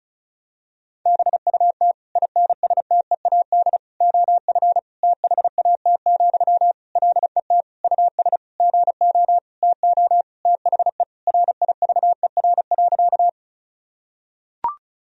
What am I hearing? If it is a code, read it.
Morse code, heard as a keyed tone, about 36 wpm, 700 Hz. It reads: BUT INSTEAD OF THAT, LET US GO TO THE RIVER.
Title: Sentences from Top 1000 Words - Review Part 4 - 35wpm